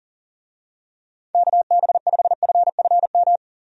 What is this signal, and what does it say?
Morse code, heard as a keyed tone, about 40 wpm, 700 Hz. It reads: KB5FFM
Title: USA Call Signs - 6 Characters - 40wpm